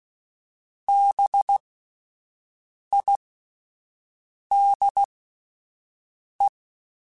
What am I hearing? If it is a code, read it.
Morse code, heard as a keyed tone, about 16 wpm, spaced as for 5 wpm, 800 Hz. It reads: BIDE